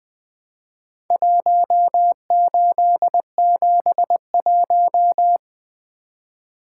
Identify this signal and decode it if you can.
Morse code, heard as a keyed tone, about 20 wpm, 700 Hz. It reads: 1871
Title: Four Letter-Number Combinations - 20wpm